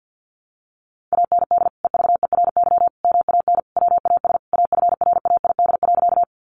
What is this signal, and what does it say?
Morse code, heard as a keyed tone, about 48 wpm, 700 Hz. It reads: AND EVERY MAN WAS AFRAID.
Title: Sentences from Top 1500 Words 50wpm